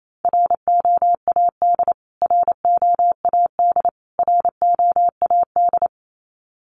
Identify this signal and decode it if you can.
Morse code, heard as a keyed tone, about 28 wpm, 700 Hz. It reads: ROAD ROAD ROAD
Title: Top 500 Words Repeated 28wpm